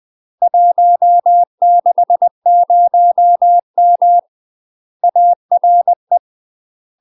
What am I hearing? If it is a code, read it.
Morse code, heard as a keyed tone, about 20 wpm, 700 Hz. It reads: 160M ARE
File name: UK MASTER QSOs TRAINING / 20WPM / UK MASTER QSOs 0026